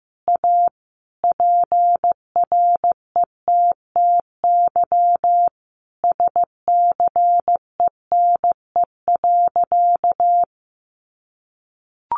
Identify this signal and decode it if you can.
Morse code, heard as a keyed tone, about 15 wpm, 700 Hz. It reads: A PRETTY SCENE.
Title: Sentences from Top 1100 Words 15wpm